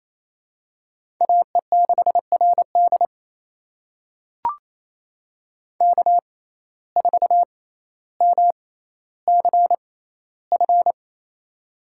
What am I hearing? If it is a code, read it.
Morse code, heard as a keyed tone, about 28 wpm, spaced as for 14 wpm, 700 Hz. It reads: AE6RD K4MCF
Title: USA Call Signs - 5 Characters - 6x Character Spacing - 28wpm